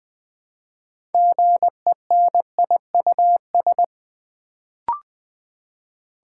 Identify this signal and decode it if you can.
Morse code, heard as a keyed tone, about 20 wpm, 700 Hz. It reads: GENIUS
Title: Build A Word - 3 Letters At A Time - 6-Letter Words - 20wpm